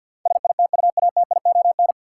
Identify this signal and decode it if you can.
Morse code, heard as a keyed tone, about 52 wpm, 700 Hz. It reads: SITUATION
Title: Sentences from Top 1200 Words - New Word Drill - 50wpm